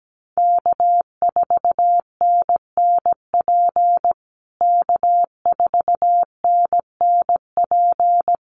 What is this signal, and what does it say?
Morse code, heard as a keyed tone, about 17 wpm, 700 Hz. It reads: K4NNP K4NNP
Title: USA Call Signs Repeated 3x 17wpm